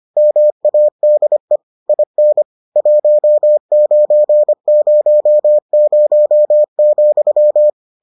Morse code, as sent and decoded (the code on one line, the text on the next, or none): -- .- -.. / .. -. / .---- ----. ----- ----- --..--
MADE IN 1900,